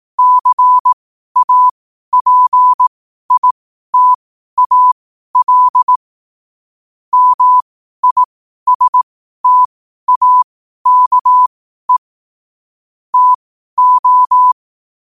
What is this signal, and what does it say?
Morse code, heard as a keyed tone, about 18 wpm, spaced as for 12 wpm, 1000 Hz. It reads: CAPITAL MISTAKE TO
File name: qotdfiles / 12wpm